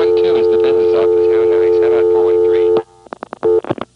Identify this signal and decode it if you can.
Dial tone (North America), 350 + 440 Hz, with speech.
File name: MoreheadCityNCNoTest